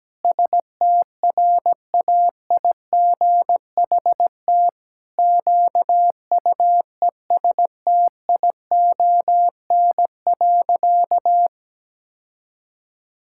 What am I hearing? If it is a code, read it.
Morse code, heard as a keyed tone, about 17 wpm, 700 Hz. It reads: STRAIGHT QUESTION.
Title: Sentences from Top 1000 Words - Review Part 6 - 17wpm